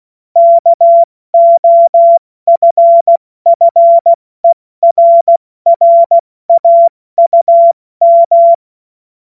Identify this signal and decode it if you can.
Morse code, heard as a keyed tone, about 16 wpm, 680 Hz. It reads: KOFFERRAUM
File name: Morsekurs / Witze / Witz 26. Oktober 2021, 70 F80